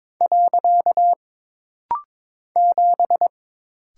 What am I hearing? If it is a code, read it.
Morse code, heard as a keyed tone, about 22 wpm, 700 Hz. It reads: . 7
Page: This rapid-fire timing is geared to push your ICR (Instant Character Recognition) skills.